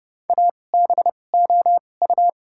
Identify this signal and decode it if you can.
Morse code, heard as a keyed tone, about 32 wpm, spaced as for 22 wpm, 700 Hz. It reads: ABOU
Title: 2x Character Spacing - Top 300 Words - 30wpm